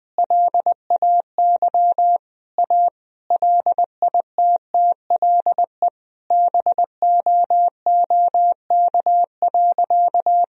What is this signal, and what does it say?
Morse code, heard as a keyed tone, about 20 wpm, 700 Hz. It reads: LAY A LITTLE BOOK.
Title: Sentences from Top 1000 Words - Review Part 2 - 20wpm